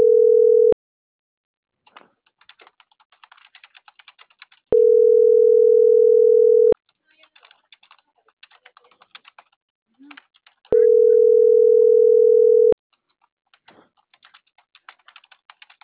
Ringback tone (North America), 440 + 480 Hz.